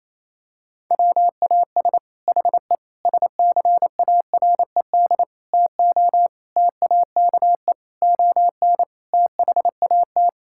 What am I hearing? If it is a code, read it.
Morse code, heard as a keyed tone, about 28 wpm, 700 Hz. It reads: WAS HE SCARED TO TAKE ON THAT
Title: Sentences from Top 1000 Words - Review Part 6 - 28wpm